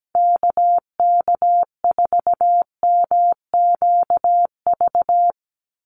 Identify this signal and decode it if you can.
Morse code, heard as a keyed tone, about 17 wpm, 700 Hz. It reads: KK4MQV
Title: USA Call Signs 17wpm